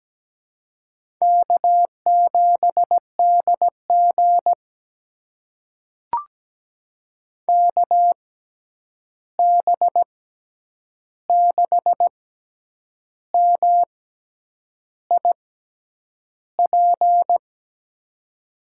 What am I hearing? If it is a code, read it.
Morse code, heard as a keyed tone, about 17 wpm, spaced as for 8 wpm, 700 Hz. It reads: K7DG KB6MIP
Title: USA Call Signs - 6x Character Spacing - 17wpm